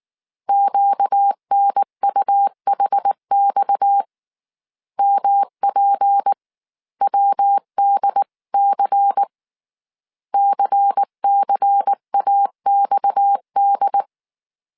Morse code, heard as a keyed tone, about 19 wpm, 800 Hz. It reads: QNUHX MPWDC CCAXD
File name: CW / DET / 1 6 1500C S C 18wpm 2356